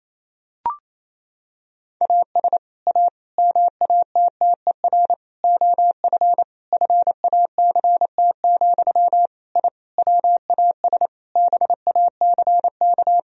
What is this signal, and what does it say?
Morse code, heard as a keyed tone, about 28 wpm, 700 Hz. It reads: AS A MATTER OF FACT, I WAS BACK
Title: Sentences from Top 1000 Words - Review Part 4 - 28wpm